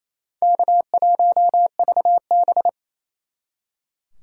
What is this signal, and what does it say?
Morse code, heard as a keyed tone, about 37 wpm, 700 Hz. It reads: K1VB